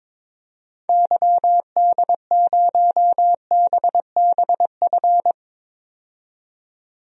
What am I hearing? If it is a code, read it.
Morse code, heard as a keyed tone, about 22 wpm, 700 Hz. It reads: YD0BBF
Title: DXCC-100 Call Signs - 6 Characters - 22wpm